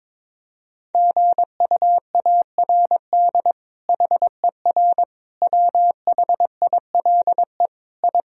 Morse code, heard as a keyed tone, about 22 wpm, 700 Hz. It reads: GUARD HER WHILE I
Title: Sentences from Top 1000 Words - Review Part 4 - 22wpm